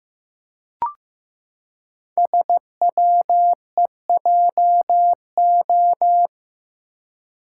Morse code, heard as a keyed tone, about 15 wpm, 700 Hz. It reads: SWEJO
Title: Code Groups - 5 Letters - 15wpm